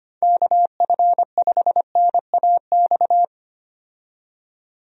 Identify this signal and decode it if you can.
Morse code, heard as a keyed tone, about 25 wpm, 700 Hz. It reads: KF5NAX
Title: USA Call Signs - 6 Characters - 25wpm